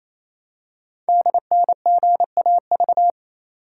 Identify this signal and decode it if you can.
Morse code, heard as a keyed tone, about 27 wpm, 700 Hz. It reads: DNGAV
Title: Code Groups - 5 Letters - 28wpm